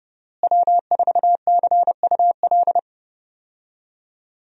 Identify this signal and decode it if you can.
Morse code, heard as a keyed tone, about 30 wpm, 700 Hz. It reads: W4CUL
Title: USA Call Signs - 5 Characters - Repeated 2x - 30wpm